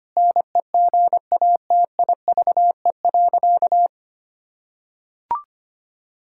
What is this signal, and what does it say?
Morse code, heard as a keyed tone, about 25 wpm, 700 Hz. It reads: NEGATIVE.
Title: Sentences from Top 1000 Words - Review Part 6 - 25wpm